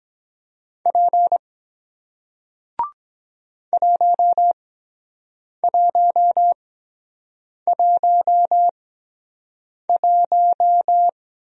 Morse code, heard as a keyed tone, about 22 wpm, spaced as for 7 wpm, 700 Hz. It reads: P 1111